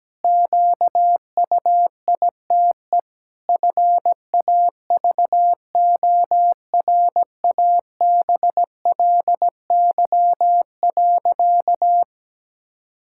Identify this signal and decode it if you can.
Morse code, heard as a keyed tone, about 17 wpm, 700 Hz. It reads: QUITE FAVORABLY.
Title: Pangram Challenge v2 - 17wpm